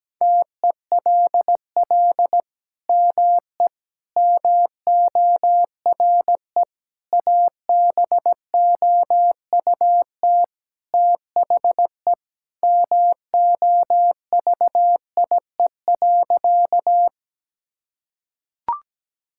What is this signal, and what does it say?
Morse code, heard as a keyed tone, about 17 wpm, 700 Hz. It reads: TELL ME MORE ABOUT THE MOVIE.
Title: Sentences from Top 1000 Words - Review Part 5 - 17wpm